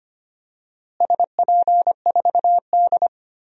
Morse code, heard as a keyed tone, about 25 wpm, 700 Hz. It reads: SP4D
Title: DXCC-100 Call Signs - 4 Characters - Repeated 3x - 25wpm